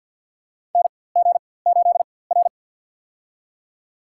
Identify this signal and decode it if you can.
Morse code, heard as a keyed tone, about 50 wpm, spaced as for 23 wpm, 700 Hz. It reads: NG8R
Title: USA Call Signs - 4x Character Spacing - 50wpm